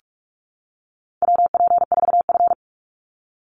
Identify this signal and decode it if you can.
Morse code, heard as a keyed tone, about 44 wpm, 700 Hz. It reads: WP4F
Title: Call Signs - 45wpm